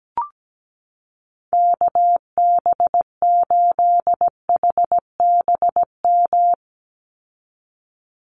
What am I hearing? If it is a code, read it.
Morse code, heard as a keyed tone, about 17 wpm, 700 Hz. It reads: KB8HBM
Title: USA Call Signs 17wpm